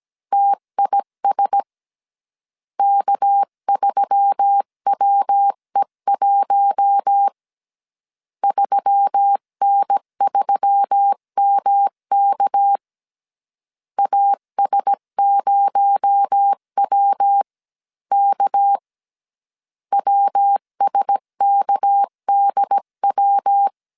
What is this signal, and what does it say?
Morse code, heard as a keyed tone, about 17 wpm, 800 Hz. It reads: TIS K3WE1 3N3MK AS0WK WSKDW
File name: CW / DET / 1 6 1500C S C 16wpm 2733